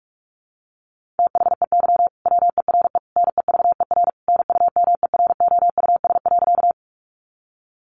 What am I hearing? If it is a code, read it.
Morse code, heard as a keyed tone, about 44 wpm, 700 Hz. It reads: THEY WERE NEVER NUMEROUS.
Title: Sentences from Top 1400 Words 45wpm